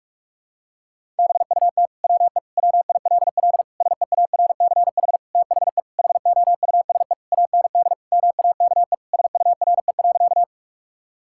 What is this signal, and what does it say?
Morse code, heard as a keyed tone, about 46 wpm, 700 Hz. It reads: BUT WE WILL SEARCH THE HOUSE AND MAKE SURE.
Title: Sentences from Top 1000 Words - Review Part 5 - 45wpm